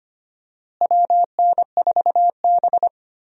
Morse code, heard as a keyed tone, about 25 wpm, 700 Hz. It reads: WN4B
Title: USA Call Signs - 4 Characters - Repeated 3x - 25wpm